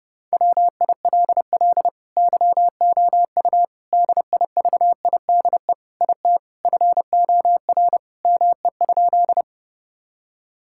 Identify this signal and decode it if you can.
Morse code, heard as a keyed tone, about 30 wpm, 700 Hz. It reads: WILL YOU DIVIDE IT FOR ME?
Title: Sentences from Top 1600 Words 30wpm